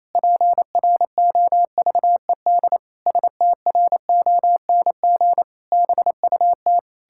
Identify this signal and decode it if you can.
Morse code, heard as a keyed tone, about 28 wpm, 700 Hz. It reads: PROVED STRONG BUT
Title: Pangram Challenge v2 - 28wpm